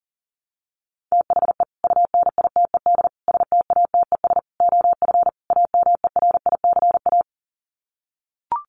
Morse code, heard as a keyed tone, about 40 wpm, 700 Hz. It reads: THE UNITED STATES OF AMERICA
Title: Sets of 5 Words 40wpm